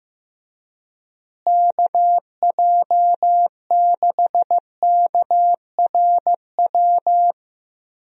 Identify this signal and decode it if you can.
Morse code, heard as a keyed tone, about 15 wpm, 700 Hz. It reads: KJ6KRW
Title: USA Call Signs - 6 Characters - Repeated 3x - 15wpm